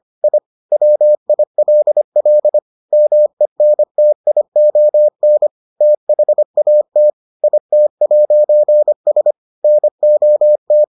Morse code, heard as a keyed tone, about 25 wpm, 600 Hz. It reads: I WILL MENTION THAT IT'S NOT